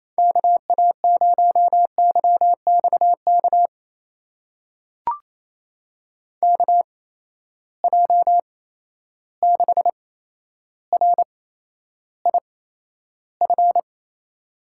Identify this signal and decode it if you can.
Morse code, heard as a keyed tone, about 28 wpm, spaced as for 12 wpm, 700 Hz. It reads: KA0YXK KJ6RIF
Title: USA Call Signs - 6 Characters - 8x Character Spacing - 28wpm